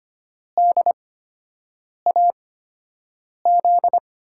Morse code, heard as a keyed tone, about 25 wpm, spaced as for 7 wpm, 700 Hz. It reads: DAZ
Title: USA Call Signs - 5 Characters - 8x Character Spacing - 25wpm